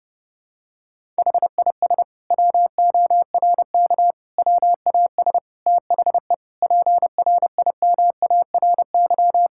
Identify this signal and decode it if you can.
Morse code, heard as a keyed tone, about 30 wpm, 700 Hz. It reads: HIS WORK WAS THE PRIMARY
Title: Sentences from Top 1000 Words - Review Part 5 - 30wpm